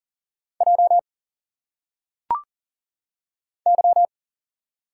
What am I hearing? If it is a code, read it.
Morse code, heard as a keyed tone, about 40 wpm, 700 Hz. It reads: J Y
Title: Single Letters - Phonetic Pronunciation 40wpm